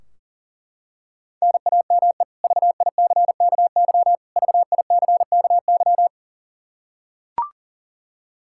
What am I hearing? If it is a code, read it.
Morse code, heard as a keyed tone, about 40 wpm, 700 Hz. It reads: NAME VICKY VICKY
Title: K1USN SST QSO Practice 40wpm